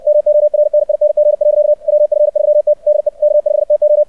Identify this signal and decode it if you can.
Morse code, heard as a keyed tone, about 64 wpm, 600 Hz. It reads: MORNING. WF3T DE K5TO
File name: archived / CRSnet